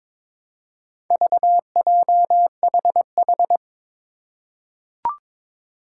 Morse code, heard as a keyed tone, about 22 wpm, 700 Hz. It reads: VJHH
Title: Four Challenging Letter-Number Combinations 22wpm